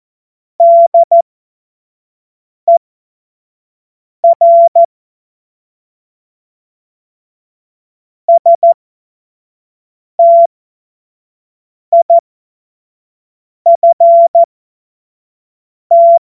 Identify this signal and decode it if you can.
Morse code, heard as a keyed tone, about 14 wpm, spaced as for 5 wpm, 680 Hz. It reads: DER STIFT
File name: Morsekurs / Witze / Witz 19 Dezember 2023 25 F70